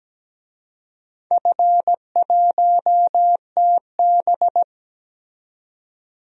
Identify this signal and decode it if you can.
Morse code, heard as a keyed tone, about 17 wpm, 700 Hz. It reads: F1TB
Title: DXCC-100 Call Signs - 4 Characters - 17wpm